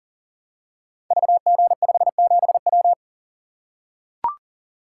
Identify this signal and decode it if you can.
Morse code, heard as a keyed tone, about 38 wpm, 700 Hz. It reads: VG57W
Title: Code Groups - 5 Letter:Numbers - 40wpm